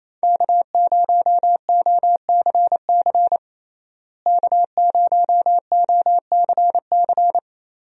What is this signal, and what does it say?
Morse code, heard as a keyed tone, about 28 wpm, 700 Hz. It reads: K0OCC K0OCC
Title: USA Call Signs - 5 Characters - Repeated 2x - 28wpm